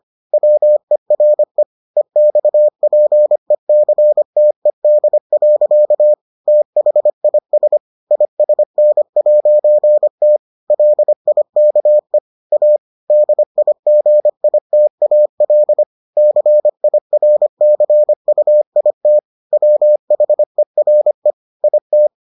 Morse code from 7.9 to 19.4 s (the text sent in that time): ISN'T LIKE A DIGITAL CIRCUIT